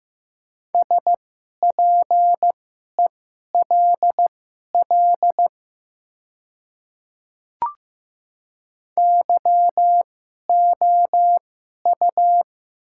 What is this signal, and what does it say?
Morse code, heard as a keyed tone, about 15 wpm, spaced as for 11 wpm, 700 Hz. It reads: SPELL YOU
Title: 2x Character Spacing - Top 200 Words - 15wpm